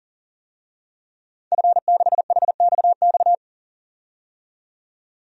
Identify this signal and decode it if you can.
Morse code, heard as a keyed tone, about 40 wpm, 700 Hz. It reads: F6HXX
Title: DXCC-100 Call Signs - 5 Characters - 40wpm